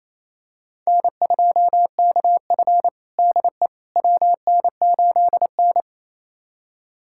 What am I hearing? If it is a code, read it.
Morse code, heard as a keyed tone, about 28 wpm, 700 Hz. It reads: N2KF DE WN8N
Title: USA Callsign DE Callsign - 4 Characters - 28wpm